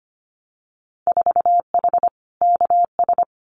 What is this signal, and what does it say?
Morse code, heard as a keyed tone, about 25 wpm, 700 Hz. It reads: 4H KS
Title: Winter Field Day Exchanges Encore 25wpm